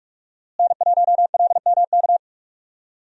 Morse code, heard as a keyed tone, about 47 wpm, 700 Hz. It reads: N1LMK
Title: USA Call Signs - 5 Characters - Repeated 2x - 45wpm